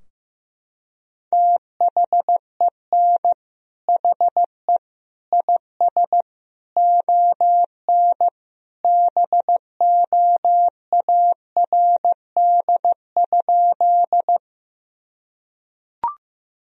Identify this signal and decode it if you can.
Morse code, heard as a keyed tone, about 15 wpm, 700 Hz. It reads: THEN HE IS ON BOARD?
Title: Sentences from Top 1000 Words - Review Part 4 - 15wpm